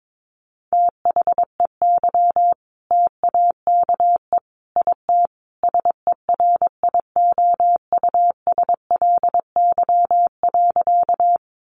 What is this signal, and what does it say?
Morse code, heard as a keyed tone, about 22 wpm, 700 Hz. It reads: THEY TAKE IT SERIOUSLY.
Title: Sentences from Top 1200 Words 22wpm